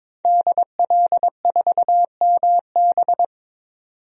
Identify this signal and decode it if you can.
Morse code, heard as a keyed tone, about 22 wpm, 700 Hz. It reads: DL4MB
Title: DXCC-100 Call Signs - 5 Characters - 22wpm